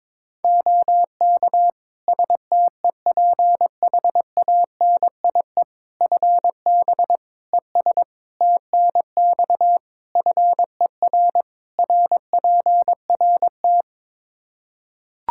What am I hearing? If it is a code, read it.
Morse code, heard as a keyed tone, about 22 wpm, 700 Hz. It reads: OK STEPHANIE FB ES TNX FER RPRT
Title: Practice QSO - Exchange 2 - Info - 22wpm